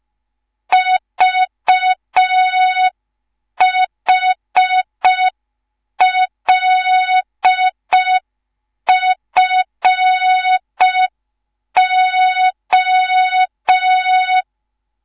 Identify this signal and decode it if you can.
Morse code, heard as a keyed tone, about 5 wpm, 3700 Hz. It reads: VHLFO
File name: Morze-HA2MN / hang / 10GWZ HA2MN